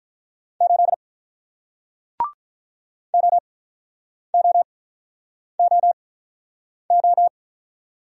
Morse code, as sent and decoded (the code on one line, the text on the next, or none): ---.. / --- --- --- ---
8 OOOO